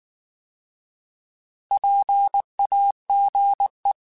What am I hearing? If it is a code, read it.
Morse code, heard as a keyed tone, about 19 wpm, 800 Hz. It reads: PAGE
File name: Alphabetical / Alphabetical mp3s / Ps @18